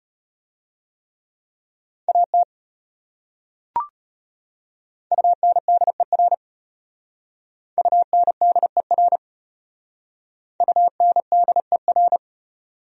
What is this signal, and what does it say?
Morse code, heard as a keyed tone, about 33 wpm, 700 Hz. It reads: AT UNDER UNDER UNDER